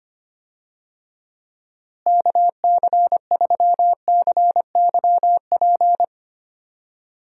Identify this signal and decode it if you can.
Morse code, heard as a keyed tone, about 25 wpm, 700 Hz. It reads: KC3CYP
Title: USA Call Signs - 6 Characters - Repeated 3x - 25wpm